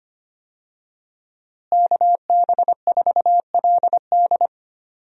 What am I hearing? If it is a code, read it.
Morse code, heard as a keyed tone, about 25 wpm, 700 Hz. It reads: KB4LD
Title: USA Call Signs - 5 Characters - Repeated 2x - 25wpm